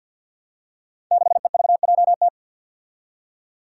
Morse code, heard as a keyed tone, about 52 wpm, 700 Hz. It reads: 6EVJT
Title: Five Challenging Letter-Number Combinations 50wpm